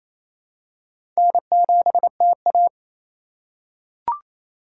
Morse code, heard as a keyed tone, about 28 wpm, 700 Hz. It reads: N7TA
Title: USA Call Signs - 4 Characters - 2x Character Spacing - 28wpm